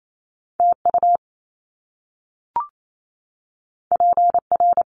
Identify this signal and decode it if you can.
Morse code, heard as a keyed tone, about 28 wpm, 700 Hz. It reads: TU PR
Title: Top Two Letter Combinations in Words 28wpm